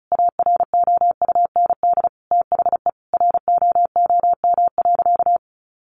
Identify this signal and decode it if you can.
Morse code, heard as a keyed tone, about 34 wpm, 700 Hz. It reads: AROUND THE ROOM.
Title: Sentences from Top 600 Words 35wpm